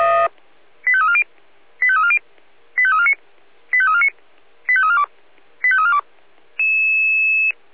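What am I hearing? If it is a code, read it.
Five-tone selective call (ZVEI), heard as quick runs of five tones, tones of about 70 ms. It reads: 85340 85340 85349 85349 85342 85342
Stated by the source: ZVEI five-tone selective call received on SDR.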